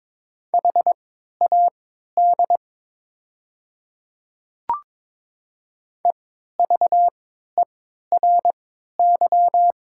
Morse code, heard as a keyed tone, about 22 wpm, spaced as for 13 wpm, 700 Hz. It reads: HAD EVERY
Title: 3x Character Spacing - Top 200 Words - 22wpm